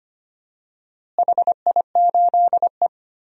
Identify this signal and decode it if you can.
Morse code, heard as a keyed tone, about 25 wpm, 700 Hz. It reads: HI8E
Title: DXCC-100 Call Signs - 4 Characters - Repeated 3x - 25wpm